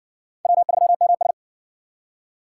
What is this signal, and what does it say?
Morse code, heard as a keyed tone, about 60 wpm, 700 Hz. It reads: W3MS